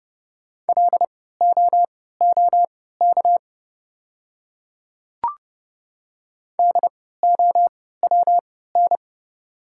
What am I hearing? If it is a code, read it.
Morse code, heard as a keyed tone, about 30 wpm, spaced as for 17 wpm, 700 Hz. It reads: LOOK DOWN
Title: 3x Character Spacing - Top 100 Words - 30wpm